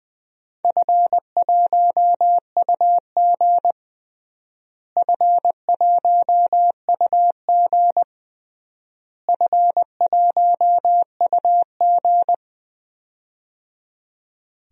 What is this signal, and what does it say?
Morse code, heard as a keyed tone, about 20 wpm, 700 Hz. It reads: F1UG F1UG F1UG
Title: DXCC-100 Call Signs - 4 Characters - Repeated 3x - 20wpm